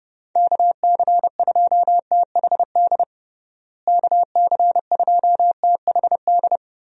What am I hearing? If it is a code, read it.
Morse code, heard as a keyed tone, about 30 wpm, 700 Hz. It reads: KC2THD KC2THD
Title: USA Call Signs - 6 Characters - Repeated 3x - 30wpm